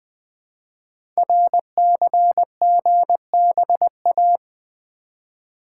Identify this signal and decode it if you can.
Morse code, heard as a keyed tone, about 20 wpm, 700 Hz. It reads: RCGBA
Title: Code Groups - 5 Letters - 20wpm